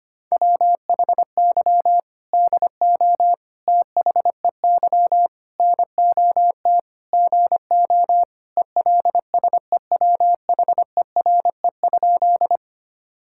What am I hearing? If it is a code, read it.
Morse code, heard as a keyed tone, about 25 wpm, 700 Hz. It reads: WHY DO THEY NOT GO ELSEWHERE?
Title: Sentences from Top 1600 Words 25wpm